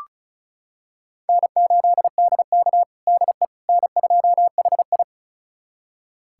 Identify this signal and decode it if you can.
Morse code, heard as a keyed tone, about 36 wpm, 700 Hz. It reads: N8DK DE N2HI
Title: USA Callsign DE Callsign - 4 Characters - 35wpm